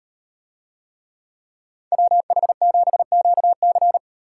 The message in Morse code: WH7QC

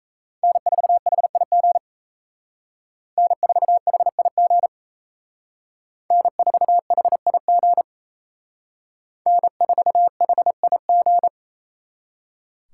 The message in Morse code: N4HIG N4HIG N4HIG N4HIG

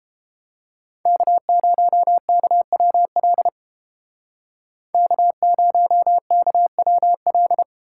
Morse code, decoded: K0KWL K0KWL

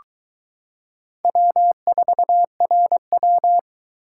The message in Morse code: W4RW

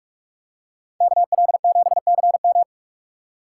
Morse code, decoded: KL7CM

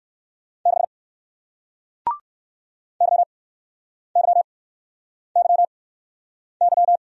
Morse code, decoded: 6 YYYY

